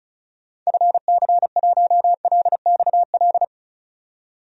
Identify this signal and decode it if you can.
Morse code, heard as a keyed tone, about 36 wpm, 700 Hz. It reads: FC1LXL
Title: DXCC-100 Call Signs - 6 Characters - 35wpm